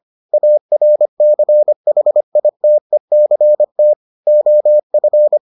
Morse code, heard as a keyed tone, about 25 wpm, 600 Hz. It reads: ARCHITECT OF